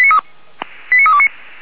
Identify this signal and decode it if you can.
Five-tone selective call (ZVEI), heard as quick runs of five tones, tones of about 70 ms. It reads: 89329